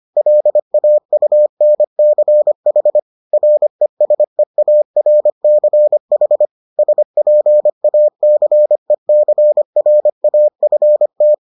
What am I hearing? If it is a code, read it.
Morse code, heard as a keyed tone, about 25 wpm, 600 Hz. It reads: LAUNCH RESEARCH SPACECRAFT